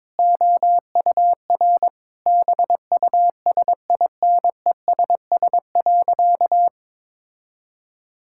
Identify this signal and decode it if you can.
Morse code, heard as a keyed tone, about 22 wpm, 700 Hz. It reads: OUR BUSINESS.
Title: Sentences from Top 1000 Words - Review Part 6 - 22wpm